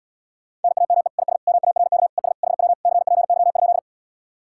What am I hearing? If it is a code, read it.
Morse code, heard as a keyed tone, about 77 wpm, 700 Hz. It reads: NAME IS MARK ES HW CPY?